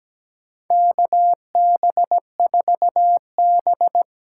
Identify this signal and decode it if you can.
Morse code, heard as a keyed tone, about 17 wpm, 700 Hz. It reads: KB4B
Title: USA Call Signs - 6 Characters - 8x Character Spacing - 17wpm